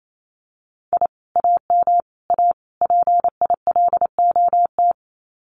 Morse code, heard as a keyed tone, about 28 wpm, 700 Hz. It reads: I AM A PILOT
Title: Common QSO Phrases Encore 28wpm